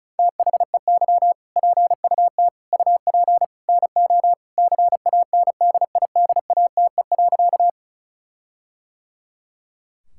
Morse code, heard as a keyed tone, about 34 wpm, 700 Hz. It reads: THEY PUT UP NO CANDIDATE.
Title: Sentences from Top 1600 Words 35wpm